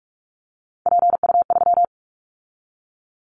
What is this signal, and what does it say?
Morse code, heard as a keyed tone, about 44 wpm, 700 Hz. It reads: PU3
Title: Callsign Prefixes for Top 100 DXCC - 45wpm